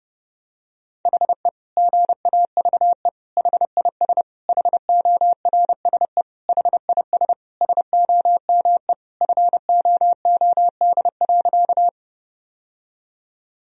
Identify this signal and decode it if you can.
Morse code, heard as a keyed tone, about 28 wpm, 700 Hz. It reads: HE GAVE HIS HORSE HIS SOME FOOD.
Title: Sentences from Top 1000 Words - Review Part 1 - 30wpm